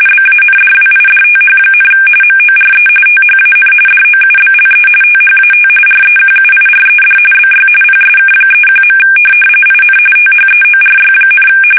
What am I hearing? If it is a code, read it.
RTTY, about 45.45 baud. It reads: BANDWIDTH, MORE THAN DOUBLE THAT REQUIRED BY PSK31. IN THEORY,
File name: fsk850 / teletype-RTTY